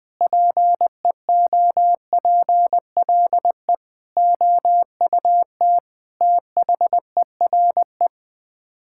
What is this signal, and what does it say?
Morse code, heard as a keyed tone, about 20 wpm, 700 Hz. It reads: PEOPLE OUT THERE
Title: Sets of 8 Words - 20wpm